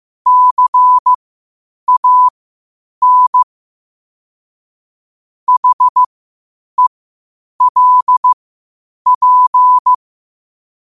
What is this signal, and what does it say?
Morse code, heard as a keyed tone, about 15 wpm, spaced as for 8 wpm, 1000 Hz. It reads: CAN HELP
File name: qotdfiles / 8wpm